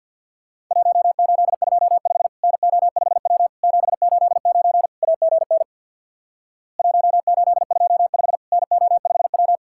Morse code, heard as a keyed tone, about 50 wpm, 700 Hz. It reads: 1825 NO5W 789 AGN 1825 NO5W
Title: Worked All Europe DX Contest - QTC Exchange with Repeats - 50wpm